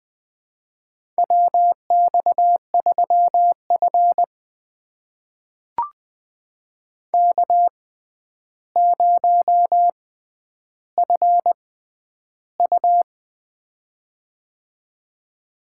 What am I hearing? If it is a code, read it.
Morse code, heard as a keyed tone, about 20 wpm, spaced as for 10 wpm, 700 Hz. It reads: WX3F K0FU
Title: USA Call Signs - 4 Characters - 6x Character Spacing - 20wpm